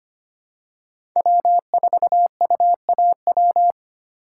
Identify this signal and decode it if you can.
Morse code, heard as a keyed tone, about 25 wpm, 700 Hz. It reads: W4UAW
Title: USA Call Signs - 5 Characters - 2x Character Spacing - 25wpm